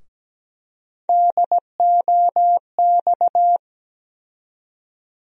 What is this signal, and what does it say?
Morse code, heard as a keyed tone, about 17 wpm, 700 Hz. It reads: DOX
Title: USA Call Sign Postfixes - 3 Letters 17wpm